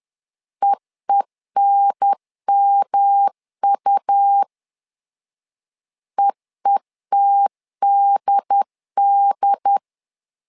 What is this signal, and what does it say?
Morse code, heard as a keyed tone, about 10 wpm, 800 Hz. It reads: EENMU EETDD
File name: LEZIONI / 3 MWRO / 3 M 10wpm 1828